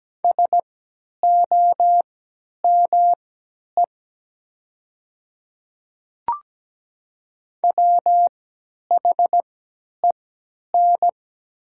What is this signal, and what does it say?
Morse code, heard as a keyed tone, about 17 wpm, spaced as for 10 wpm, 700 Hz. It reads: SOME WHEN
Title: 3x Character Spacing - Top 200 Words - 17wpm